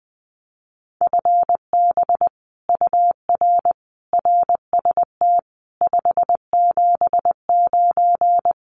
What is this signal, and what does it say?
Morse code, heard as a keyed tone, about 19 wpm, 700 Hz. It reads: FB UR RST 579
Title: Common QSO Phrases Encore 20wpm